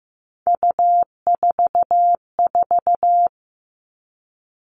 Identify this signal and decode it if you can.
Morse code, heard as a keyed tone, about 15 wpm, 700 Hz. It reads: U44
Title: All Cut Numbers - 3-Digit 15wpm